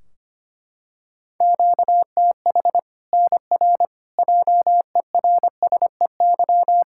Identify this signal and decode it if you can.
Morse code, heard as a keyed tone, about 25 wpm, 700 Hz. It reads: QTH NR JERSEY
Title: K1USN SST QSO Practice 25wpm